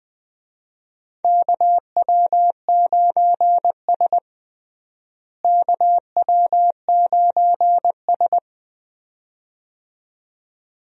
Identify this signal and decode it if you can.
Morse code, heard as a keyed tone, about 20 wpm, 700 Hz. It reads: KW9S KW9S
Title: USA Call Signs - 4 Characters - Repeated 2x - 20wpm